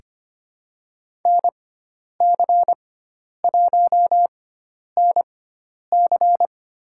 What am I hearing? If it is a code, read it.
Morse code, heard as a keyed tone, about 25 wpm, spaced as for 10 wpm, 700 Hz. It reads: NC1NC
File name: Lesson 16g - Letter c - Call Signs 25wpm